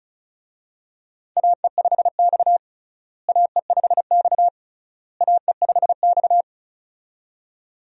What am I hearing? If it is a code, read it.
Morse code, heard as a keyed tone, about 34 wpm, 700 Hz. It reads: AE5X AE5X AE5X
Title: USA Call Signs - 4 Characters - Repeated 3x - 35wpm